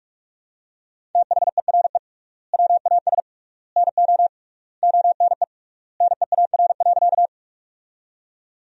Morse code, heard as a keyed tone, about 42 wpm, 700 Hz. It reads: THERE WAS NO ONE NEAR.
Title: 3x Word-Spacing - Sentences from Top 200 Words - 45wpm